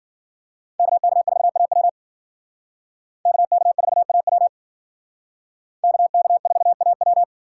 Morse code, heard as a keyed tone, about 48 wpm, 690 Hz. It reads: KK4AW KK4AW KK4AW
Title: Speed Racing - USA Call Signs 40wpm